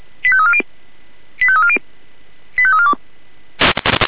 Five-tone selective call (ZVEI), heard as quick runs of five tones, tones of about 70 ms. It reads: 85340 85340 85342